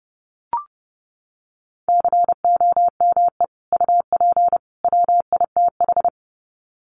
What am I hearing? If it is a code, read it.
Morse code, heard as a keyed tone, about 30 wpm, 700 Hz. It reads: COME UP WITH
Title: Sets of 3 Words 30wpm